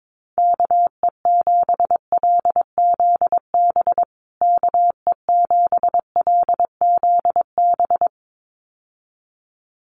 Morse code, heard as a keyed tone, about 22 wpm, 700 Hz. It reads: KE7LZB KE7LZB
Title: USA Call Signs Repeated 2x 22wpm